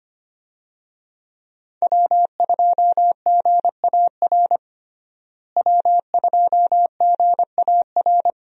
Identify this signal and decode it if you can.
Morse code, heard as a keyed tone, about 25 wpm, 700 Hz. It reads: W2GAR W2GAR
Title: USA Call Signs - 5 Characters - Repeated 2x - 25wpm